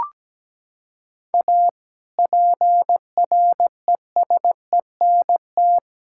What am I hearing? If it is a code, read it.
Morse code, heard as a keyed tone, about 17 wpm, 700 Hz. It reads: A PRESENT
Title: Sentences from Top 1000 Words - Review Part 5 - 17wpm